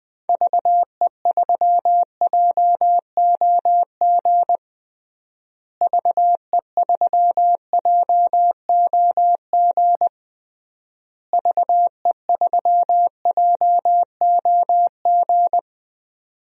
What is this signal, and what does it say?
Morse code, heard as a keyed tone, about 20 wpm, 700 Hz. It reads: VE3JOG VE3JOG VE3JOG
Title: DXCC-100 Call Signs - 6 Characters - Repeated 3x - 20wpm